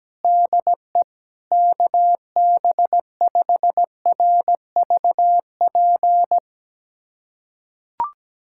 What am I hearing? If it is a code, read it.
Morse code, heard as a keyed tone, about 17 wpm, 700 Hz. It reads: DE KB5RVP
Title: USA Callsign DE Callsign 17wpm